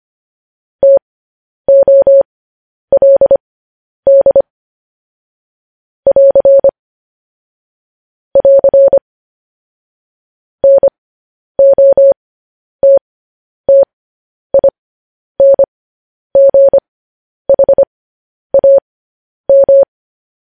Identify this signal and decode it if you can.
Morse code, heard as a keyed tone, about 25 wpm, spaced as for 10 wpm, 550 Hz. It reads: TOLD <AR> <AR> NOTTINGHAM
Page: News Headlines in Morse Code at 10 WPM